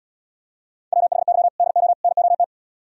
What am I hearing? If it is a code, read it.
Morse code, heard as a keyed tone, about 77 wpm, 700 Hz. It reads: WHY DO NOT